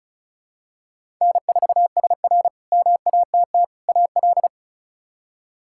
Morse code, heard as a keyed tone, about 34 wpm, 700 Hz. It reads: N4SR MATT AL
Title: K1USN SST QSO Practice - 35wpm